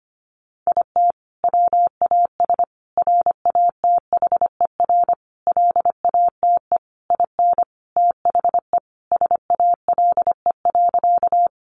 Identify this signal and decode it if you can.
Morse code, heard as a keyed tone, about 25 wpm, 700 Hz. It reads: IT WAS RATHER LATE IN THE SALE.
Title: Sentences from Top 1400 Words 25wpm